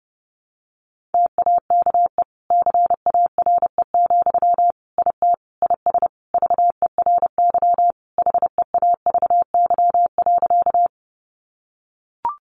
Morse code, heard as a keyed tone, about 30 wpm, 700 Hz. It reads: TAKE CARE, IT IS VERY HEAVY.
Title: Sentences from Top 1100 Words 30wpm